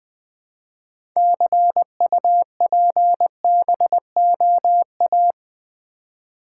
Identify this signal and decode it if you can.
Morse code, heard as a keyed tone, about 20 wpm, 700 Hz. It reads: CUPBOA
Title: Build A Word - 2 Letters At A Time - 8-Letter Words - 20wpm